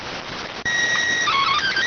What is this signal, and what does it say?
SSTV (slow-scan television), PD 290.